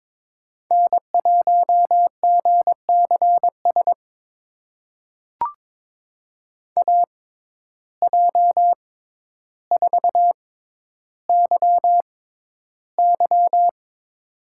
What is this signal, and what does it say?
Morse code, heard as a keyed tone, about 22 wpm, spaced as for 11 wpm, 700 Hz. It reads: N1GCS AJ4YY
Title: USA Call Signs - 6x Character Spacing - 22wpm